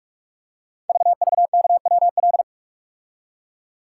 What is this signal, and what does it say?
Morse code, heard as a keyed tone, about 44 wpm, 700 Hz. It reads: VVKWL
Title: Code Groups - 5 Letters - 45wpm